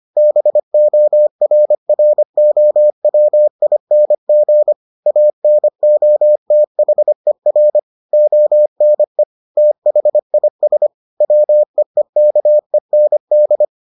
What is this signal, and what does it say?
Morse code, heard as a keyed tone, about 25 wpm, 600 Hz. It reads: BORROWING ANOTHER ONE THIS WEEKEND